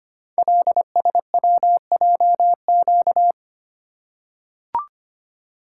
Morse code, heard as a keyed tone, about 25 wpm, 700 Hz. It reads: LSWJQ
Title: Code Groups - 5 Letter:Numbers - 25wpm